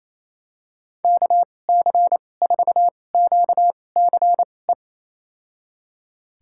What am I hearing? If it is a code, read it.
Morse code, heard as a keyed tone, about 28 wpm, spaced as for 20 wpm, 700 Hz. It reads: KC4QCE
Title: USA Call Signs - 2x Character Spacing - 28wpm